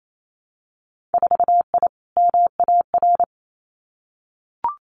Morse code, heard as a keyed tone, about 28 wpm, 700 Hz. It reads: 4I MAR
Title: Winter Field Day Exchanges Encore 28wpm